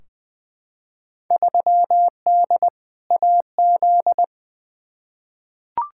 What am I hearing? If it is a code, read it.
Morse code, heard as a keyed tone, about 20 wpm, 700 Hz. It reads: 3D AZ
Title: Field Day Exchanges - 20wpm